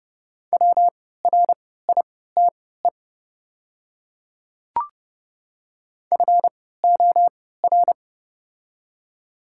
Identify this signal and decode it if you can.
Morse code, heard as a keyed tone, about 29 wpm, spaced as for 17 wpm, 700 Hz. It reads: WRITE FOR
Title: 3x Character Spacing - Top 300 Words - 30wpm